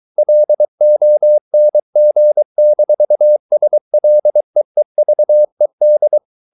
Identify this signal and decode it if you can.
Morse code, heard as a keyed tone, about 23 wpm, 600 Hz. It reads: LONG-SLEEVED
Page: Romanen Never af Ken Follet på engelsk.